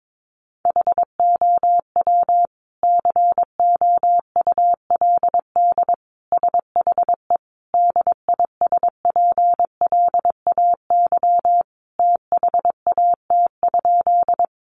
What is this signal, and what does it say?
Morse code, heard as a keyed tone, about 22 wpm, 700 Hz. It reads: HOW COULD SHE DISPLAY THAT?
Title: Sentences from Top 1200 Words 22wpm